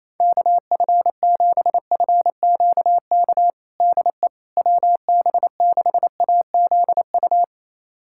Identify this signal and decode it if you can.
Morse code, heard as a keyed tone, about 28 wpm, 700 Hz. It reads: KF7FQK DE WB6AZU
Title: USA Callsign DE Callsign - 6 Characters - 28wpm